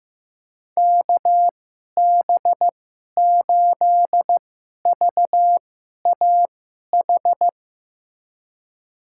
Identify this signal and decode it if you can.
Morse code, heard as a keyed tone, about 15 wpm, spaced as for 11 wpm, 700 Hz. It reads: KB8VAH
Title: USA Call Signs - 6 Characters - 2x Character Spacing - 15wpm